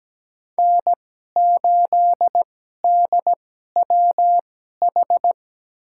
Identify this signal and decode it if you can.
Morse code, heard as a keyed tone, about 17 wpm, spaced as for 12 wpm, 700 Hz. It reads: N8DWH
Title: USA Call Signs - 5 Characters - 2x Character Spacing - 17wpm